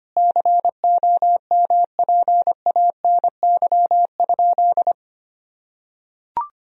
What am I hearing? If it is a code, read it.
Morse code, heard as a keyed tone, about 25 wpm, 700 Hz. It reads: COMPANY?
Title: Sentences from Top 1000 Words - Review Part 3 - 25wpm